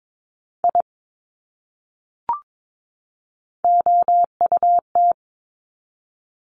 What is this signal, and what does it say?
Morse code, heard as a keyed tone, about 22 wpm, 700 Hz. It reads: I OUT
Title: Top 200 Words 22wpm